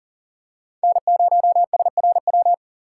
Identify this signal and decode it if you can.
Morse code, heard as a keyed tone, about 40 wpm, 700 Hz. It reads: N0SRW
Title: USA Call Signs - 5 Characters - 2x Character Spacing - 40wpm